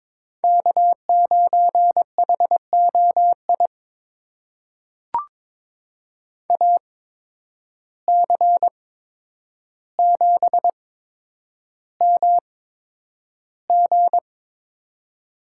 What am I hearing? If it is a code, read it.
Morse code, heard as a keyed tone, about 22 wpm, spaced as for 9 wpm, 700 Hz. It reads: K9HOI AC7MG
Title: USA Call Signs - 5 Characters - 8x Character Spacing - 22wpm